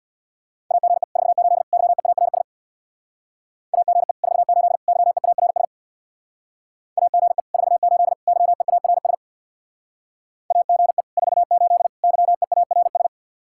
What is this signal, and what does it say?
Morse code, heard as a keyed tone, about 61 wpm, 700 Hz. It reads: AGE 48 YEARS AGE 48 YEARS AGE 48 YEARS AGE 48 YEARS
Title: Speed-Racing Common QSO Phrases - 50wpm